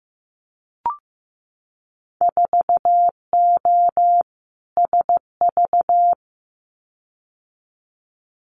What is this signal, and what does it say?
Morse code, heard as a keyed tone, about 15 wpm, 700 Hz. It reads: 4O SV
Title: Winter Field Day Exchanges Encore 15wpm